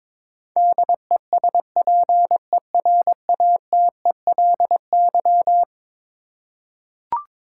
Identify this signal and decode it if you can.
Morse code, heard as a keyed tone, about 22 wpm, 700 Hz. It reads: DESPERATELY
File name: practice / 22wpm / Eleven Letter Words - 22wpm